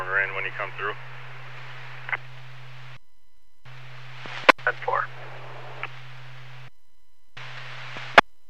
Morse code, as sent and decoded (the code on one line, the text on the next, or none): none